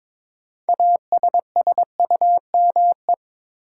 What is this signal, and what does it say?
Morse code, heard as a keyed tone, about 22 wpm, 700 Hz. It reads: ASSUME
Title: Sentences from Top 1500 Words - New Word Drill - 22wpm